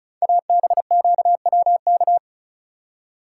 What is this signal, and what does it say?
Morse code, heard as a keyed tone, about 36 wpm, 700 Hz. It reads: ABQWK
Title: Code Groups - 5 Letters - 35wpm